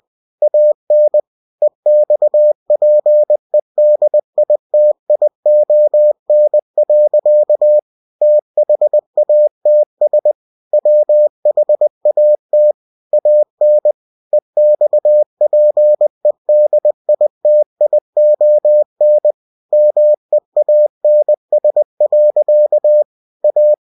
Morse code, heard as a keyed tone, about 20 wpm, 600 Hz. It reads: AN EXPEDITION. THATS WHAT AN EXPEDITION MEANS. A
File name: cw-books / winnie-the-pooh / winnie-20wpm-page-0077